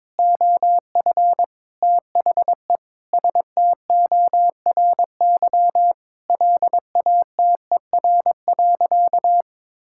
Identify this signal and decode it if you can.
Morse code, heard as a keyed tone, about 22 wpm, 700 Hz. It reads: OF THE STORY LATER.
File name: practice / 22wpm / Sentences with 4-Word Phrases - 22wpm